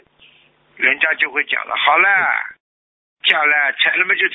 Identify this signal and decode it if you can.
Emergency Alert System (SAME) tones, heard as harsh buzzing bursts, with speech. It is heard from 0.8 to 3.8 s.